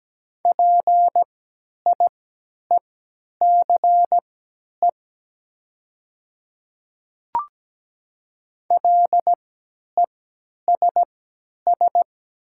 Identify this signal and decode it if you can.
Morse code, heard as a keyed tone, about 17 wpm, spaced as for 10 wpm, 700 Hz. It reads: PIECE LESS
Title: 3x Character Spacing - Top 500 Words - 17wpm